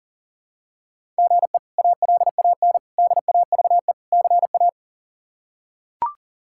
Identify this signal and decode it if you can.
Morse code, heard as a keyed tone, about 37 wpm, 700 Hz. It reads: GE ALAN DAVE CA
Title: K1USN SST QSO Practice - 40wpm